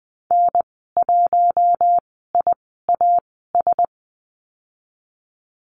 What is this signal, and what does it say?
Morse code, heard as a keyed tone, about 20 wpm, spaced as for 14 wpm, 700 Hz. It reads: N1IAS
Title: 042 Lesson 8g - Number 1 - Call Signs 20wpm